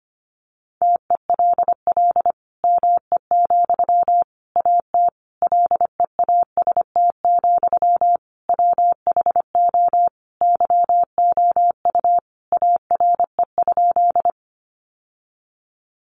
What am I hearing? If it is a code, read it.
Morse code, heard as a keyed tone, about 25 wpm, 700 Hz. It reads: TELL ME, AT LEAST, WHO YOU ARE?
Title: Sentences from Top 700 Words 25wpm